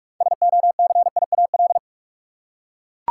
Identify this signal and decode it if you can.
Morse code, heard as a keyed tone, about 43 wpm, 700 Hz. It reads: SOCIAL
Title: Practice QSO - Exchange 3 - More Info - 45wpm